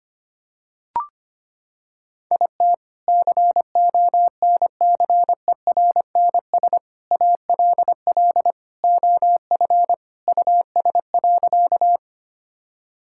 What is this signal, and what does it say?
Morse code, heard as a keyed tone, about 25 wpm, 700 Hz. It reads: IT CONCERNS ALL OF US.
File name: practice / 25wpm / Sentences with 3-Word Phrases - 25wpm